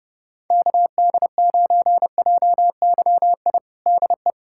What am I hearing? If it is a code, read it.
Morse code, heard as a keyed tone, about 30 wpm, 700 Hz. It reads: KD9JYI DE
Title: USA Callsign DE Callsign 30wpm